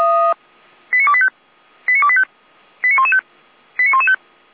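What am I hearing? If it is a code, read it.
Five-tone selective call (ZVEI), heard as quick runs of five tones, tones of about 70 ms. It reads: 89285 89285 89115 89115
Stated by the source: ZVEI five-tone call received by SDR.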